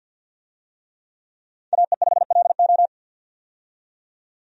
Morse code, heard as a keyed tone, about 48 wpm, 700 Hz. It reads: AE5LO
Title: USA Call Signs - 5 Characters - 50wpm